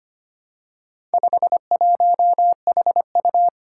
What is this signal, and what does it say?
Morse code, heard as a keyed tone, about 25 wpm, 700 Hz. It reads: 51HU
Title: Four Challenging Letter-Number Combinations 25wpm